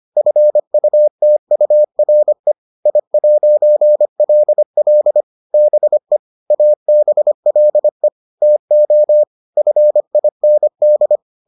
Morse code, heard as a keyed tone, about 25 wpm, 600 Hz. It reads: FUTURE I'LL BE ABLE TO FIND